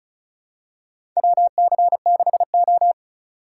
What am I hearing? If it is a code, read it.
Morse code, heard as a keyed tone, about 36 wpm, 700 Hz. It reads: WC6O
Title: USA Call Signs - 4 Characters - 3x Character Spacing - 35wpm